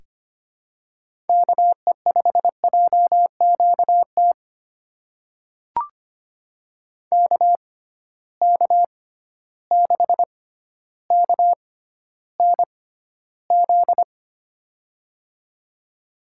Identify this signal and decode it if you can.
Morse code, heard as a keyed tone, about 25 wpm, spaced as for 13 wpm, 700 Hz. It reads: KE5JQT KK6KNZ
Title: USA Call Signs - 6 Characters - 6x Character Spacing - 25wpm